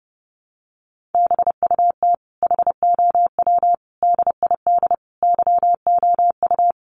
Morse code, heard as a keyed tone, about 30 wpm, 700 Hz. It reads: BUT HOW DID YOU
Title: Sentences from Top 1400 Words 30wpm